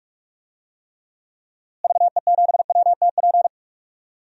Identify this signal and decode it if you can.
Morse code, heard as a keyed tone, about 46 wpm, 700 Hz. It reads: VE7WTP
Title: DXCC-100 Call Signs - 6 Characters - Repeated 3x - 45wpm